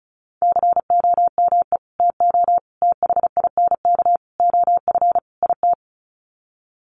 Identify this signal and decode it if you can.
Morse code, heard as a keyed tone, about 34 wpm, 700 Hz. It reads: COME TO THINK OF IT
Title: Sets of 5 Words Encore 35wpm